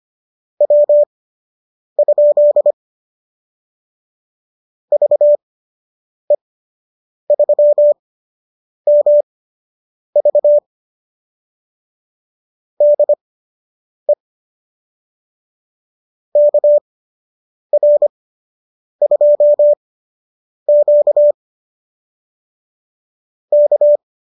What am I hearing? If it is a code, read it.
Morse code, heard as a keyed tone, about 25 wpm, spaced as for 8 wpm, 590 Hz. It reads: W? VE3MV DE KR2Q K